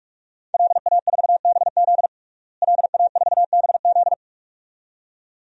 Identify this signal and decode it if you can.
Morse code, heard as a keyed tone, about 46 wpm, 700 Hz. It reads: LA4BZ LA4BZ
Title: DXCC-100 Call Signs - 5 Characters - Repeated 2x - 45wpm